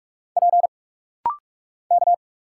Morse code, heard as a keyed tone, about 46 wpm, 700 Hz. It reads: P K
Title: Single Letter-Number Alt - Rapid-Fire 45wpm